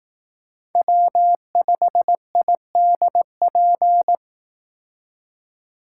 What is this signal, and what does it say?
Morse code, heard as a keyed tone, about 18 wpm, 700 Hz. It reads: W5IDP
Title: Speed Racing - USA Call Signs 15wpm